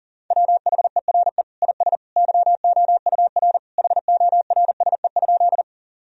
Morse code, heard as a keyed tone, about 40 wpm, 700 Hz. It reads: WHERE IS YOUR HORSE?
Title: Sentences from Top 1000 Words - Review Part 1 - 40wpm